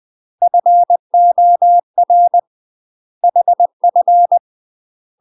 Morse code, heard as a keyed tone, about 20 wpm, 700 Hz. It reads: FOR HF
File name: UK MASTER QSOs TRAINING / 20WPM / UK MASTER QSOs 0034